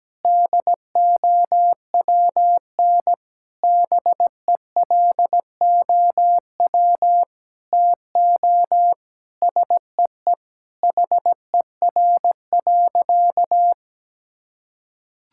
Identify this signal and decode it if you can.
Morse code, heard as a keyed tone, about 17 wpm, 700 Hz. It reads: DOWN BELOW TO SEE HER.
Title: Sentences from Top 1000 Words - Review Part 5 - 17wpm